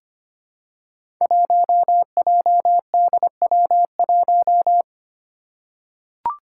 Morse code, heard as a keyed tone, about 25 wpm, 700 Hz. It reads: 1JDW1
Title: Five Challenging Letter-Number Combinations 25wpm